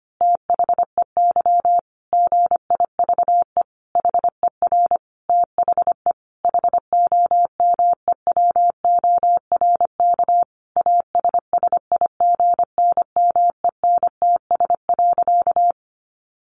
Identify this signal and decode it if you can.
Morse code, heard as a keyed tone, about 25 wpm, 700 Hz. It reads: THEY GIVE HER THE HOMEWORK ASSIGNMENTS.
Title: Easy Phrases 25wpm